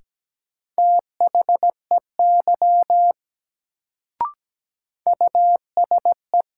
Morse code, heard as a keyed tone, about 17 wpm, 700 Hz. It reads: THEY USE
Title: Sentences from Top 100 Words - New Word Drill - 17wpm